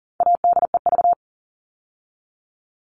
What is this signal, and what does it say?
Morse code, heard as a keyed tone, about 40 wpm, 700 Hz. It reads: ADEV
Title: All Cut Numbers - 4-Digit 40wpm